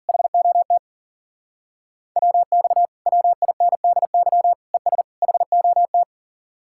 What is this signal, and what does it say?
Morse code, heard as a keyed tone, about 52 wpm, 700 Hz. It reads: HOT WX WINDY ES HOT